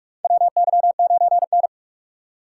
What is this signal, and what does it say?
Morse code, heard as a keyed tone, about 44 wpm, 700 Hz. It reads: WY9N
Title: USA Call Signs - 4 Characters - 4x Character Spacing - 45wpm